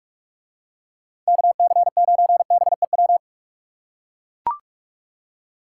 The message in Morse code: KC9BEW

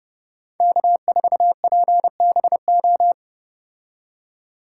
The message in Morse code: K4PBO

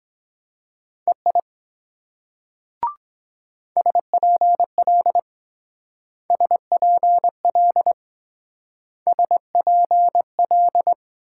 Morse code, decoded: EI SPL SPL SPL

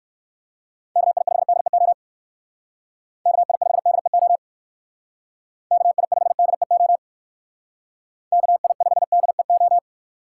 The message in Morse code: KI5DEO KI5DEO KI5DEO KI5DEO

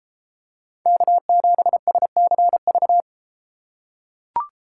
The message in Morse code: K7SCV

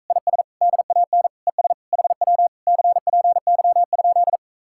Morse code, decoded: IS DAN ES HW CPY?

